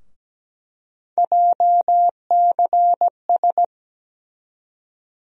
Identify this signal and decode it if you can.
Morse code, heard as a keyed tone, about 17 wpm, 700 Hz. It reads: JCS
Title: USA Call Sign Postfixes - 3 Letters 17wpm